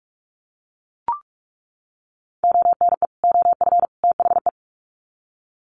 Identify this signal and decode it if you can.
Morse code, heard as a keyed tone, about 44 wpm, 700 Hz. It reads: ONE OF THE
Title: Sets of 3 Words 45wpm